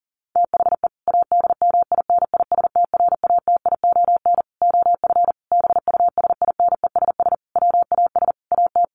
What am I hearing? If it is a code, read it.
Morse code, heard as a keyed tone, about 40 wpm, 700 Hz. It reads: THE ADMINISTRATION OF BUSINESS WAS AT
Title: Sentences from Top 1300 Words 40wpm